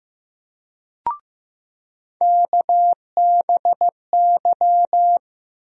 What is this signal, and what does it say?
Morse code, heard as a keyed tone, about 15 wpm, 700 Hz. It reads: KBY
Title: USA Call Sign Postfixes - 3 Letters 15wpm